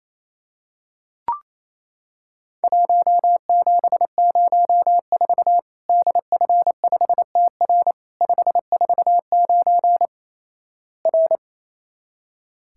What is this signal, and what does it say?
Morse code, heard as a keyed tone, about 28 wpm, 700 Hz. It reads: 1704 DF5TR 549 R
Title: Worked All Europe DX Contest - QTC Exchange - 28wpm